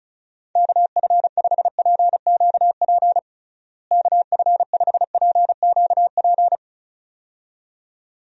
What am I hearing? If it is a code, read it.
Morse code, heard as a keyed tone, about 34 wpm, 700 Hz. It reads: KF5PQP KF5PQP
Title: USA Call Signs - 6 Characters - Repeated 2x - 35wpm